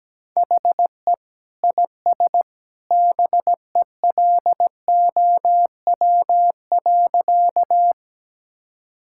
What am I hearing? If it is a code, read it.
Morse code, heard as a keyed tone, about 17 wpm, 700 Hz. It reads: HE IS BELOW.
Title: Sentences from Top 1000 Words - Review Part 5 - 17wpm